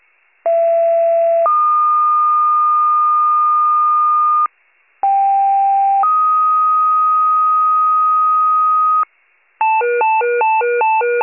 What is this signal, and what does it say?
Two-tone page, heard as a one-second tone, then a three-second tone, about 670 then 1220 Hz, about 790 then 1250 Hz. It is heard from 0.4 to 9.0 s.